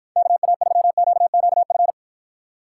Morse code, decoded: KA3QQF

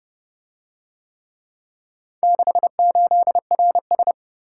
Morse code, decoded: 68RS

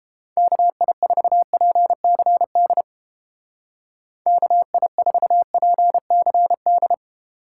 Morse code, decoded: KI4PCD KI4PCD

